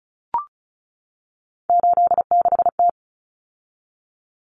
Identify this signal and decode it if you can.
Morse code, heard as a keyed tone, about 34 wpm, 700 Hz. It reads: 86T
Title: Common Cut Numbers - 3-Digit 35wpm